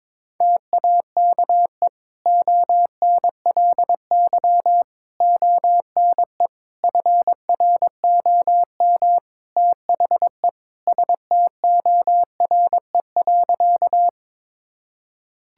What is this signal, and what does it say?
Morse code, heard as a keyed tone, about 21 wpm, 700 Hz. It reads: TAKE ONLY ONE FROM THE STORE.
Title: Sentences from Top 1000 Words - Review Part 5 - 22wpm